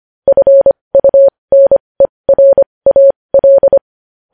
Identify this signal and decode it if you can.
Morse code, heard as a keyed tone, about 25 wpm, spaced as for 20 wpm, 550 Hz. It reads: FUNERAL
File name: podcast / News headlines-1x-25x20